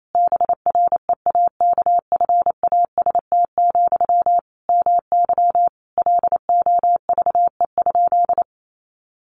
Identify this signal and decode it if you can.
Morse code, heard as a keyed tone, about 28 wpm, 700 Hz. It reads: BREAKFAST, MY LOVE?
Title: Sentences from Top 1500 Words 28wpm